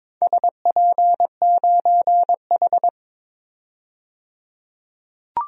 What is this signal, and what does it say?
Morse code, heard as a keyed tone, about 22 wpm, 700 Hz. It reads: SP9H
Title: DXCC-100 Call Signs - 4 Characters - 22wpm